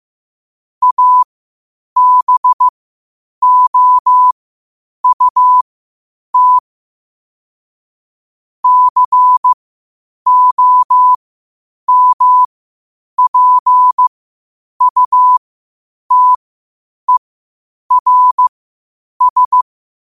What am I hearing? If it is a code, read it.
Morse code, heard as a keyed tone, about 15 wpm, spaced as for 8 wpm, 1000 Hz. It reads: ABOUT COMPUTERS
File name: qotdfiles / 8wpm